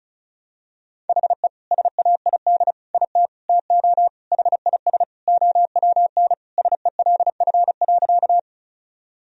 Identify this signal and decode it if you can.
Morse code, heard as a keyed tone, about 34 wpm, 690 Hz. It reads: HE SAID IT TO HIS OWN SELF.
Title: Sentences from Top 1000 Words - Review Part 1 - 35wpm